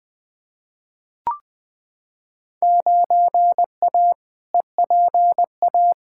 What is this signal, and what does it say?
Morse code, heard as a keyed tone, about 20 wpm, 700 Hz. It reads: 9A EPA
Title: Field Day Exchanges Encore - 20wpm